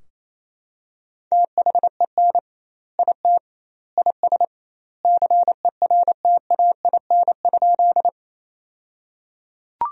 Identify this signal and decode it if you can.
Morse code, heard as a keyed tone, about 28 wpm, 700 Hz. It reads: THEN IT IS CERTAIN?
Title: 2x Word-Spacing - Sentences from Top 500 Words - 28wpm